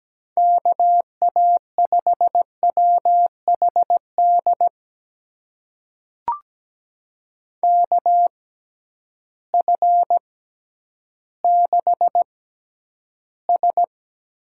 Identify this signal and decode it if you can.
Morse code, heard as a keyed tone, about 17 wpm, spaced as for 10 wpm, 700 Hz. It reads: KA5WHD KF6S
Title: USA Call Signs - 6 Characters - 6x Character Spacing - 17wpm